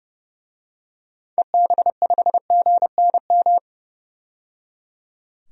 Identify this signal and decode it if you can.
Morse code, heard as a keyed tone, about 30 wpm, 700 Hz. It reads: EB5GNM
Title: DXCC-100 Call Signs - 6 Characters - 30wpm